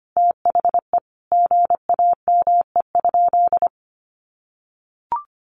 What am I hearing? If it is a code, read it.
Morse code, heard as a keyed tone, about 25 wpm, 700 Hz. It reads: THE GAME?
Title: Sentences from Top 1400 Words 25wpm